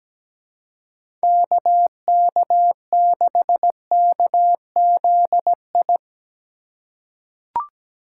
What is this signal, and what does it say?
Morse code, heard as a keyed tone, about 17 wpm, 700 Hz. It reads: KK6KZI
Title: USA Call Signs - 6 Characters - 2x Character Spacing - 17wpm